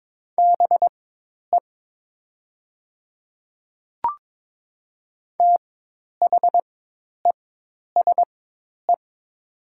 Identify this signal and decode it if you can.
Morse code, heard as a keyed tone, about 22 wpm, spaced as for 10 wpm, 700 Hz. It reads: BE THESE
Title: 4x Character Spacing - Top 100 Words - 22wpm